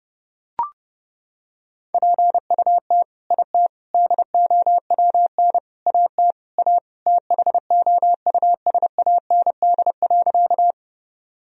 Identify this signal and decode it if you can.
Morse code, heard as a keyed tone, about 30 wpm, 700 Hz. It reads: PUT IT DOWN AT A THOUSAND.
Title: Sentences from Top 1000 Words - Review Part 3 - 30wpm